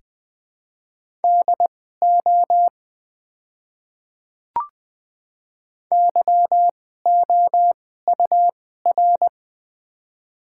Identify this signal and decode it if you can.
Morse code, heard as a keyed tone, about 20 wpm, spaced as for 14 wpm, 700 Hz. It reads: DO YOUR
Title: 2x Character Spacing - Top 100 Words - 20wpm